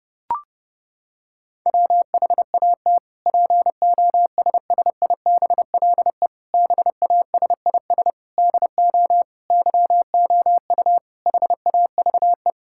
Morse code, heard as a keyed tone, about 30 wpm, 700 Hz. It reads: WHAT POSSIBLE BASIS DO YOU HAVE
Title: Sentences from Top 1000 Words - Review Part 5 - 30wpm